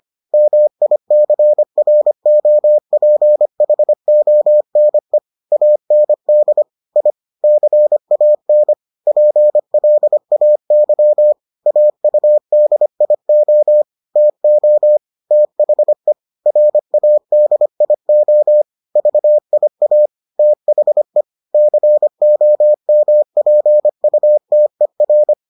Morse code, heard as a keyed tone, about 25 wpm, 600 Hz. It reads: MICROPHONE AND I CAN PLAY AUDIO TO THE RADIO VIA THE COMPUTER